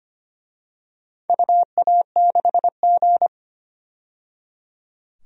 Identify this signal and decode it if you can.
Morse code, heard as a keyed tone, about 25 wpm, 700 Hz. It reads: UA6G
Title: DXCC-100 Call Signs - 4 Characters - 25wpm